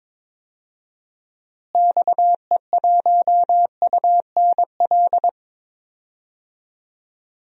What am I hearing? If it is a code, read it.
Morse code, heard as a keyed tone, about 22 wpm, 700 Hz. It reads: XE1UNL
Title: DXCC-100 Call Signs - 6 Characters - Repeated 3x - 22wpm